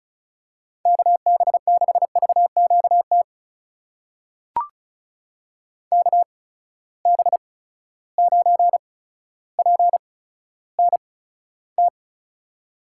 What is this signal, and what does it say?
Morse code, heard as a keyed tone, about 36 wpm, spaced as for 15 wpm, 700 Hz. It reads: KB6VQT KB9PNT
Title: USA Call Signs - 6 Characters - 8x Character Spacing - 35wpm